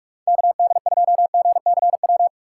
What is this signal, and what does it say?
Morse code, heard as a keyed tone, about 46 wpm, 700 Hz. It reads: KD2GCW
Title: USA Call Signs - 6 Characters - 4x Character Spacing - 45wpm